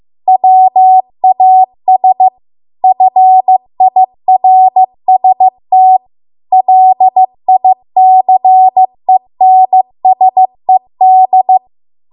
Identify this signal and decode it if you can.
Morse code, heard as a keyed tone, about 15 wpm, 750 Hz. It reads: WAS FIRST LICENSED
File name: Archive / 15 WPM / 15WPM